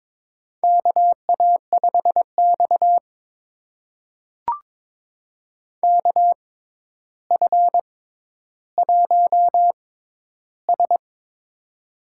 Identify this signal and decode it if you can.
Morse code, heard as a keyed tone, about 22 wpm, spaced as for 11 wpm, 700 Hz. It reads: KA5X KF1S
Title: USA Call Signs - 4 Characters - 6x Character Spacing - 22wpm